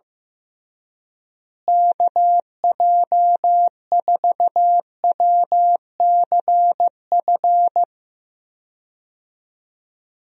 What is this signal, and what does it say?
Morse code, heard as a keyed tone, about 14 wpm, 700 Hz. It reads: KJ4WCF
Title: USA Call Signs - 6 Characters - Repeated 3x - 15wpm